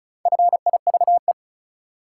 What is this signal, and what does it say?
Morse code, heard as a keyed tone, about 36 wpm, 700 Hz. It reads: FIVE
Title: Sets of 5 Words - New Word Drill - 35wpm